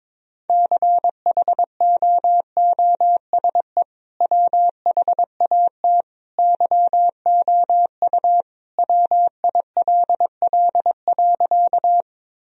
Morse code, heard as a keyed tone, about 22 wpm, 700 Hz. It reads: CHOOSE WHAT YOU WILL.
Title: Sentences from Top 1000 Words - Review Part 4 - 22wpm